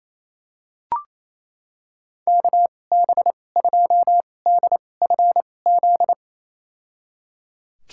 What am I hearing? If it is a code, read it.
Morse code, heard as a keyed tone, about 28 wpm, spaced as for 20 wpm, 700 Hz. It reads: KB2DFZ
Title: USA Call Signs - 2x Character Spacing - 28wpm